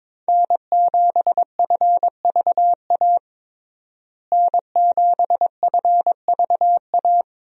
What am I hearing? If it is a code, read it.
Morse code, heard as a keyed tone, about 22 wpm, 700 Hz. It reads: N7FVA N7FVA
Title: USA Call Signs - 5 Characters - Repeated 2x - 22wpm